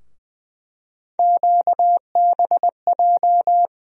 Morse code, heard as a keyed tone, about 20 wpm, 700 Hz. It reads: QBJ
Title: USA Call Sign Postfixes - 3 Letters 20wpm